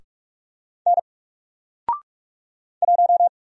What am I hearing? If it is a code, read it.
Morse code, heard as a keyed tone, about 44 wpm, 700 Hz. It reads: N 1
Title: Single Letter-Number 45wpm